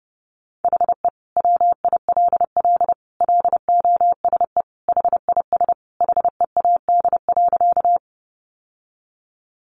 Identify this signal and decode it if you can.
Morse code, heard as a keyed tone, about 30 wpm, 700 Hz. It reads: HE WILL LOSE HIS HEAD.
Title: Sentences from Top 1100 Words 30wpm